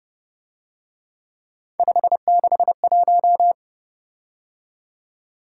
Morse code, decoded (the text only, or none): 561